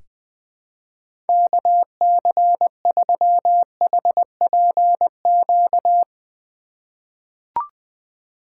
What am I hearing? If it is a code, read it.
Morse code, heard as a keyed tone, about 20 wpm, 700 Hz. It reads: KC3HPQ
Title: USA Call Signs - 6 Characters - 2x Character Spacing - 20wpm